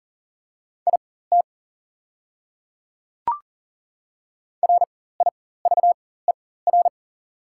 Morse code, heard as a keyed tone, about 40 wpm, spaced as for 19 wpm, 690 Hz. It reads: IT RIVER
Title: 4x Character Spacing - Top 300 Words - 40wpm